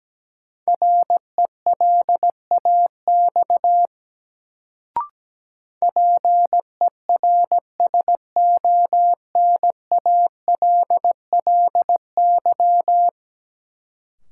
Morse code, heard as a keyed tone, about 17 wpm, 700 Hz. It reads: RELAX PERSONALLY
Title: Sentences from Top 1400 Words - New Word Drill Alt - 17wpm